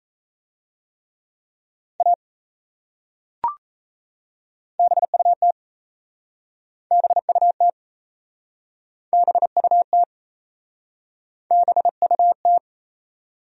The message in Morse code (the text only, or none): A BUT BUT BUT BUT